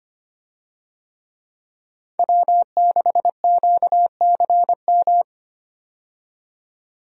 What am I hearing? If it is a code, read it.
Morse code, heard as a keyed tone, about 25 wpm, 700 Hz. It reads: W6QCM
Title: USA Call Signs - 5 Characters - 25wpm